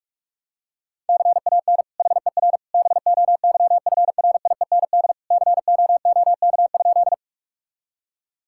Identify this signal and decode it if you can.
Morse code, heard as a keyed tone, about 46 wpm, 700 Hz. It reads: CAN HER BOYFRIEND COOK?
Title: Sentences from Top 1000 Words - Review Part 5 - 45wpm